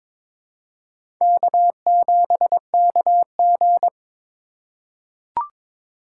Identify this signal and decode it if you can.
Morse code, heard as a keyed tone, about 22 wpm, 700 Hz. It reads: K7KG
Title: USA Call Signs - 4x Character Spacing - 22wpm